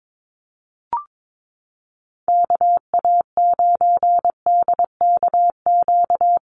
Morse code, heard as a keyed tone, about 22 wpm, 700 Hz. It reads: KA9DKQ
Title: USA Call Signs Repeated 3x 22wpm